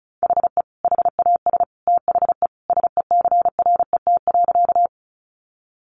Morse code, heard as a keyed tone, about 34 wpm, 700 Hz. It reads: HE HAS THE SECRET.
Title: Sentences from Top 1200 Words 35wpm